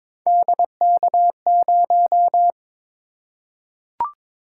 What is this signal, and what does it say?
Morse code, heard as a keyed tone, about 22 wpm, 700 Hz. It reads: DK0
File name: practice / 22wpm / Three Letter-Number Combinations - 22wpm